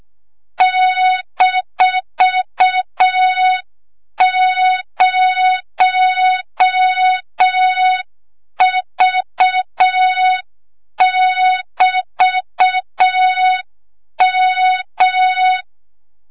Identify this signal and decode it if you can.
Morse code, heard as a keyed tone, about 6 wpm, 3700 Hz. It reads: -0V<BT>M
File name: Morze-HA2MN / hang / 25 GYAKORLAT